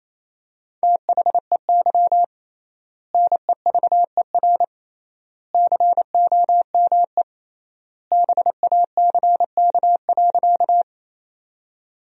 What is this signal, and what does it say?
Morse code, heard as a keyed tone, about 27 wpm, 700 Hz. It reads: THEY NEVER COME BACK.
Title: 3x Word-Spacing - Sentences from Top 300 Words - 28wpm